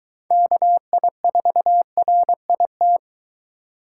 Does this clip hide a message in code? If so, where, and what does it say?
Yes: Morse code from 0.3 to 3.0 s, heard as a keyed tone, about 23 wpm, 700 Hz. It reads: KI4RIT